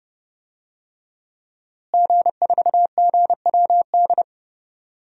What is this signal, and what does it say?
Morse code, heard as a keyed tone, about 30 wpm, 700 Hz. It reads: G4GWD
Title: DXCC-100 Call Signs - 5 Characters - Repeated 2x - 30wpm